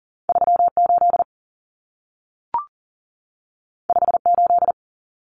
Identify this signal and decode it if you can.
Morse code, heard as a keyed tone, about 40 wpm, 700 Hz. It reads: 38 58
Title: Common Cut Numbers - 2-Digit 40wpm